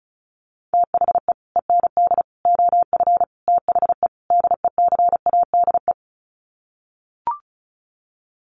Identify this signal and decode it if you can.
Morse code, heard as a keyed tone, about 34 wpm, 700 Hz. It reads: THE END OF THE DECADE
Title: Sets of 5 Words Encore 35wpm